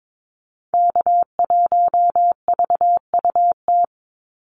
Morse code, heard as a keyed tone, about 22 wpm, 700 Hz. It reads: K1VUT
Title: Call Signs - 22wpm